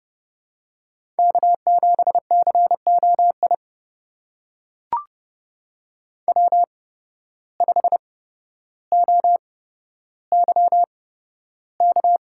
Morse code, heard as a keyed tone, about 30 wpm, spaced as for 13 wpm, 700 Hz. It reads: K7COI W5OYK
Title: USA Call Signs - 5 Characters - 8x Character Spacing - 30wpm